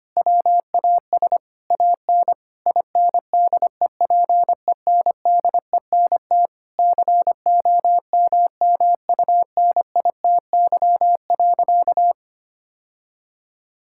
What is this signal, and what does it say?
Morse code, heard as a keyed tone, about 25 wpm, 700 Hz. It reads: WAS AN INDEPENDENT COMMUNITY.
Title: Sentences from Top 1000 Words - Review Part 6 - 25wpm